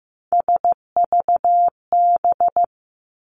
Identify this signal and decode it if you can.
Morse code, heard as a keyed tone, about 15 wpm, 700 Hz. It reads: SVB
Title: Three Challenging Letter-Number Combinations 15wpm